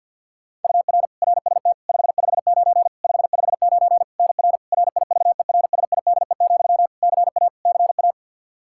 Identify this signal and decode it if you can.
Morse code, heard as a keyed tone, about 51 wpm, 700 Hz. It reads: UR RST 559 559 NR RIVERSIDE, CA CA
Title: Practice QSO - Exchange 1 - Beginning - 50wpm